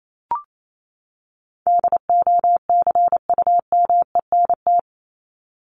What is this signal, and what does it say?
Morse code, heard as a keyed tone, about 28 wpm, 700 Hz. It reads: DOCUMENT
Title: Top 1751-2000 Words 28wpm